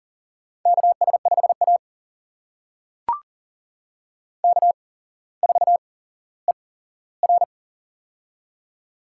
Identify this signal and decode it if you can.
Morse code, heard as a keyed tone, about 40 wpm, spaced as for 17 wpm, 700 Hz. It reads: KS5A K4ER
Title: USA Call Signs - 4 Characters - 8x Character Spacing - 40wpm